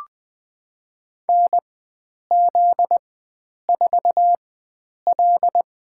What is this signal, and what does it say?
Morse code, heard as a keyed tone, about 21 wpm, spaced as for 9 wpm, 700 Hz. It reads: NZ4L
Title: USA Call Signs - 4 Characters - 4x Character Spacing - 20wpm